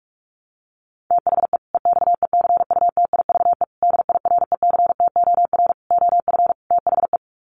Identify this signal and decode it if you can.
Morse code, heard as a keyed tone, about 44 wpm, 700 Hz. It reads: THE EXECUTIVE DIRECTOR OF THE
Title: Sets of 5 Words Encore 45wpm